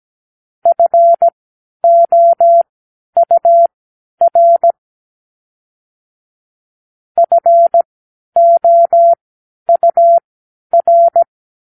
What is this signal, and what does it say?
Morse code, heard as a keyed tone, about 17 wpm, spaced as for 10 wpm, 680 Hz. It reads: FOUR FOUR
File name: arrl / Numeral Four (4)